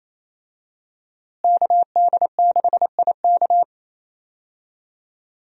Morse code, heard as a keyed tone, about 28 wpm, 700 Hz. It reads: KD6IK
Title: USA Call Signs - 5 Characters - Repeated 2x - 28wpm